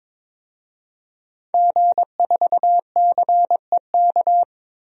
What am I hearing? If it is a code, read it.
Morse code, heard as a keyed tone, about 22 wpm, 700 Hz. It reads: G4CEK
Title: DXCC-100 Call Signs - 5 Characters - 22wpm